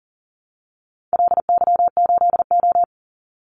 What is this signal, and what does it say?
Morse code, heard as a keyed tone, about 40 wpm, 700 Hz. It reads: LY8O
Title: Call Signs Repeated 2x - 40wpm